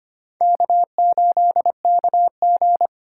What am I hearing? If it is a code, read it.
Morse code, heard as a keyed tone, about 25 wpm, 700 Hz. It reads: K8KG
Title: USA Call Signs - 4 Characters - 2x Character Spacing - 25wpm